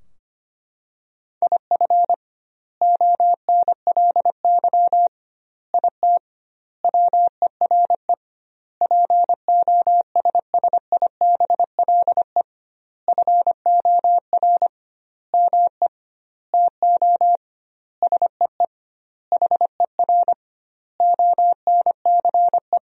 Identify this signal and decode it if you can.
Morse code, heard as a keyed tone, about 25 wpm, 700 Hz. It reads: IF ONLY IT WERE POSSIBLE FOR ME TO SEE HER ONCE
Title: 2x Word-Spacing - Sentences from Top 500 Words - 25wpm